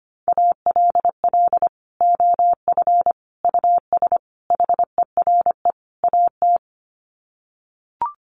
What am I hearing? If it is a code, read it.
Morse code, heard as a keyed tone, about 25 wpm, 700 Hz. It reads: ALL OF US HERE AT
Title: Sets of 5 Words 25wpm